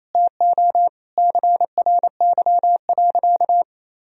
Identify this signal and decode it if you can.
Morse code, heard as a keyed tone, about 27 wpm, 700 Hz. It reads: TO CRY.
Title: Sentences from Top 1000 Words - Review Part 2 - 28wpm